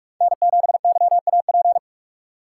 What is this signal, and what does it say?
Morse code, heard as a keyed tone, about 44 wpm, 700 Hz. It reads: N7YAP
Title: USA Call Signs - 6x Character Spacing - 45wpm